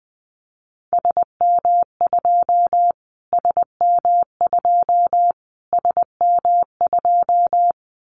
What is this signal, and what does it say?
Morse code, heard as a keyed tone, about 19 wpm, 700 Hz. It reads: SM2 SM2 SM2
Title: Callsign Prefixes for Top 25 DXCC - Repeated 3x - 20wpm